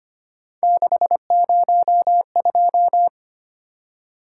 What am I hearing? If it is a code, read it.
Morse code, heard as a keyed tone, about 25 wpm, 700 Hz. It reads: 602
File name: Three Letter-Number Combinations - 25wpm